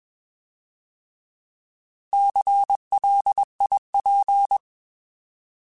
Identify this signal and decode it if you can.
Morse code, heard as a keyed tone, about 21 wpm, 800 Hz. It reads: CLIP
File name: Alphabetical / Alphabetical mp3s / Cs @20